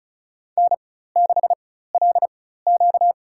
Morse code, heard as a keyed tone, about 36 wpm, spaced as for 16 wpm, 700 Hz. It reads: N6LQ
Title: USA Call Signs - 4 Characters - 4x Character Spacing - 35wpm